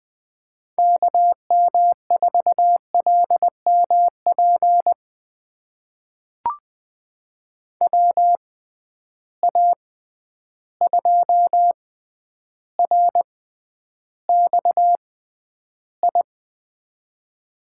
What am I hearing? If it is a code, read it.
Morse code, heard as a keyed tone, about 20 wpm, spaced as for 10 wpm, 700 Hz. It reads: KM4LMP WA2RXI
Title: USA Call Signs - 6 Characters - 6x Character Spacing - 20wpm